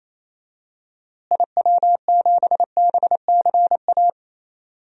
Morse code, heard as a keyed tone, about 27 wpm, 700 Hz. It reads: IW7BCA
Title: DXCC-100 Call Signs - 6 Characters - Repeated 2x - 28wpm